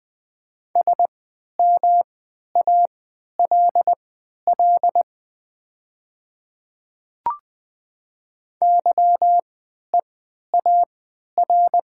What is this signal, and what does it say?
Morse code, heard as a keyed tone, about 20 wpm, spaced as for 11 wpm, 700 Hz. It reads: SMALL YEAR
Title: 3x Character Spacing - Top 300 Words - 20wpm